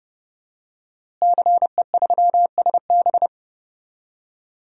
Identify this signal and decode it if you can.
Morse code, heard as a keyed tone, about 30 wpm, 700 Hz. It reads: CE3SB
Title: DXCC-100 Call Signs - 5 Characters - Repeated 2x - 30wpm